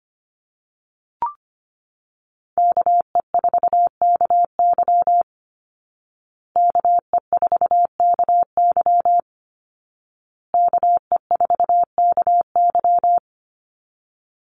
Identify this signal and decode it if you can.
Morse code, heard as a keyed tone, about 25 wpm, 700 Hz. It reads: KE4KY KE4KY KE4KY
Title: Call Signs Repeated 2x - 25wpm